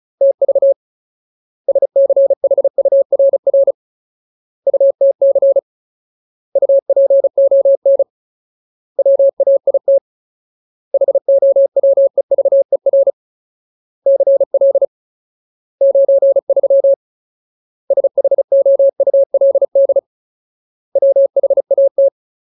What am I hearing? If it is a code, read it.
Morse code, heard as a keyed tone, about 34 wpm, 550 Hz. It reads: TV SCHURR UTC UPON WAIT HOWEVER CL 93 SHOULD WHAT
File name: Ultimate Word List / 35WPM Ultimate